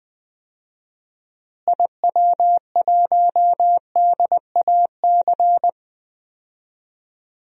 Morse code, heard as a keyed tone, about 20 wpm, 700 Hz. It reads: IW1DAC
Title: DXCC-100 Call Signs - 6 Characters - Repeated 2x - 20wpm